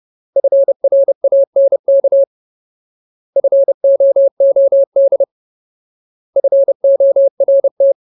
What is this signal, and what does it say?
Morse code, heard as a keyed tone, about 30 wpm, 550 Hz. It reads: FRANK FOOD FORT
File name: Ultimate Word List / 30WPM Ultimate